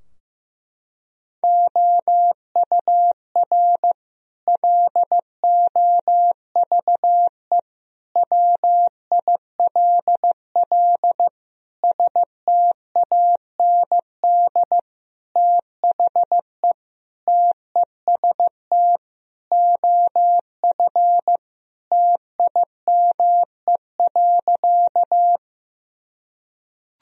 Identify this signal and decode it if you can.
Morse code, heard as a keyed tone, about 15 wpm, 700 Hz. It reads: OUR LOVE WILL STAND THE TEST OF TIME.
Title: Sentences from Top 1000 Words - Review Part 3 - 15wpm